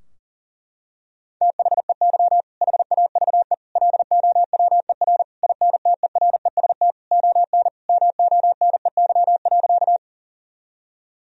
Morse code, heard as a keyed tone, about 40 wpm, 700 Hz. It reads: THEY HAVE LOWER INTEREST ON MONEY.
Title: Sentences from Top 1000 Words - Review Part 4 - 40wpm